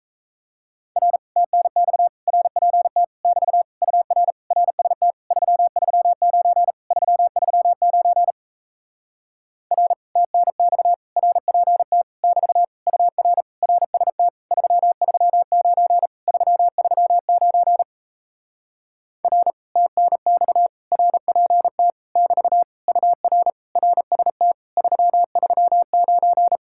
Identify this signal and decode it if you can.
Morse code, heard as a keyed tone, about 37 wpm, 700 Hz. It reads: R TNX RPT <BT> UR RST 339 339 R TNX RPT <BT> UR RST 339 339 R TNX RPT <BT> UR RST 339